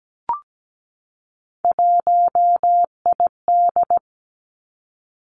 Morse code, heard as a keyed tone, about 17 wpm, 700 Hz. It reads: 1ID
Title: Three Challenging Letter-Number Combinations 17wpm